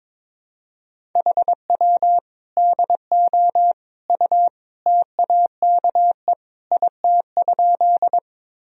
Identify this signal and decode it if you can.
Morse code, heard as a keyed tone, about 22 wpm, 700 Hz. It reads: HW DO U TAKE IT?
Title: Sentences from Top 200 Words and Abbreviations 22wpm